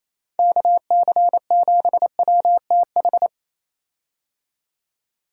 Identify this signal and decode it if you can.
Morse code, heard as a keyed tone, about 28 wpm, 700 Hz. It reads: KC7WTH
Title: USA Call Signs - 6 Characters - 28wpm